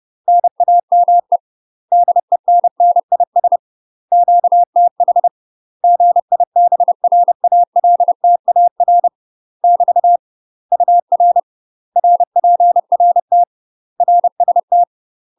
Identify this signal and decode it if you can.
Morse code, heard as a keyed tone, about 30 wpm, 700 Hz. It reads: NAME DENNIS QTH GIBRALTAR <BT> UR RPRT RST
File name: UK MASTER QSOs TRAINING / 30WPM / UK MASTER QSOs 0000